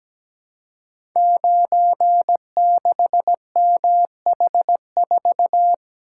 Morse code, heard as a keyed tone, about 17 wpm, 700 Hz. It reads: 96MH4
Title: Code Groups - 5 Letter:Numbers - 17wpm